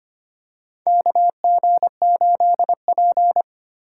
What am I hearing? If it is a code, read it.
Morse code, heard as a keyed tone, about 25 wpm, 700 Hz. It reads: KG8P
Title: USA Call Signs - 4 Characters - 8x Character Spacing - 25wpm